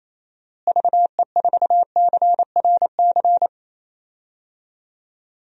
Morse code, heard as a keyed tone, about 28 wpm, 700 Hz. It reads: VE4CRC
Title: DXCC-100 Call Signs - 6 Characters - Repeated 2x - 28wpm